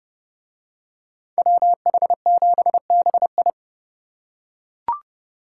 Morse code, heard as a keyed tone, about 29 wpm, 700 Hz. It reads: WH7BI